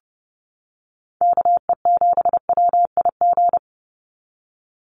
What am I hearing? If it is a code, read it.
Morse code, heard as a keyed tone, about 30 wpm, 700 Hz. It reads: KE7WIG
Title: USA Call Signs 30wpm